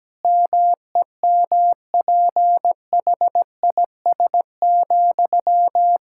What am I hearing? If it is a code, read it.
Morse code, heard as a keyed tone, about 17 wpm, 700 Hz. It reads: MEMPHIS,
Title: Practice QSO - Exchange 1 - Beginning - 17wpm